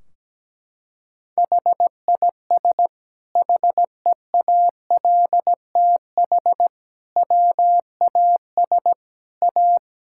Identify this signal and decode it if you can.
Morse code, heard as a keyed tone, about 17 wpm, 700 Hz. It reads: HIS HEALTH WAS A
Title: Sentences from Top 1600 Words 17wpm